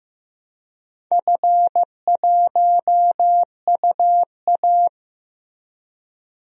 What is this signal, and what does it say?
Morse code, heard as a keyed tone, about 15 wpm, 700 Hz. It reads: F1UA
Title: DXCC-100 Call Signs - 4 Characters - 15wpm